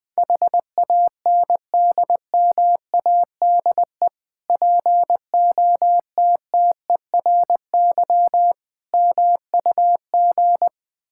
Morse code, heard as a keyed tone, about 20 wpm, 690 Hz. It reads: HANDMADE POTTERY MUG
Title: Difficult Words with Hints - 20wpm